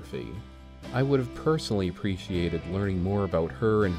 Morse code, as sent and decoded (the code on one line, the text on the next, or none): none